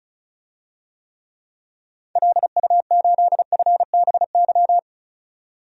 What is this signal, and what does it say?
Morse code, heard as a keyed tone, about 34 wpm, 700 Hz. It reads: LU8FBY
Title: DXCC-100 Call Signs - 6 Characters - 35wpm